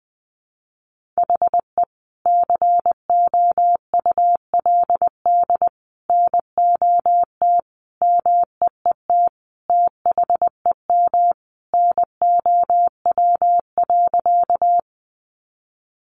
Morse code, heard as a keyed tone, about 19 wpm, 700 Hz. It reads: HE COULD NOT MEET THEM NOW.
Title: Sentences from Top 1100 Words 20wpm